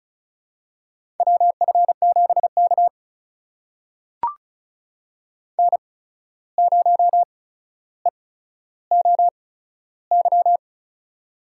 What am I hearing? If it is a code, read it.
Morse code, heard as a keyed tone, about 36 wpm, spaced as for 14 wpm, 700 Hz. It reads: WF7K N0EOY
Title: USA Call Signs - 8x Character Spacing - 35wpm